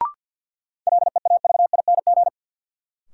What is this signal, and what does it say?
Morse code, heard as a keyed tone, about 49 wpm, 700 Hz. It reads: LEAVING
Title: Sentences from Top 1000 Words - Review Part 4 - New Word Drill - 50wpm